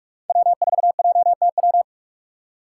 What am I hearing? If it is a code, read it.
Morse code, heard as a keyed tone, about 42 wpm, 700 Hz. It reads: W4JTW
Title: USA Call Signs - 5 Characters - 2x Character Spacing - 45wpm